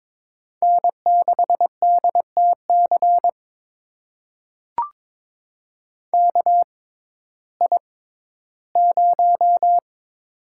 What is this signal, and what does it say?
Morse code, heard as a keyed tone, about 23 wpm, spaced as for 14 wpm, 700 Hz. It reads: N6DTC KI0
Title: USA Call Signs - 5 Characters - 6x Character Spacing - 22wpm